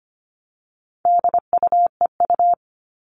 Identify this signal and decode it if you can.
Morse code, heard as a keyed tone, about 25 wpm, 700 Hz. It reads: DUEU
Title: All Cut Numbers - 4-Digit 25wpm